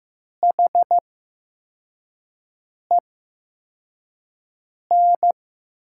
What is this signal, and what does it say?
Morse code, heard as a keyed tone, about 15 wpm, spaced as for 4 wpm, 700 Hz. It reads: HEN
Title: 8x Character Spacing - Top 100 Words - 15wpm